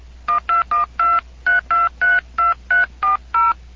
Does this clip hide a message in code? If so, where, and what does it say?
Yes: DTMF from 0.3 to 3.5 s, heard as telephone touch tones, tones of about 150 ms. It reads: 12123232340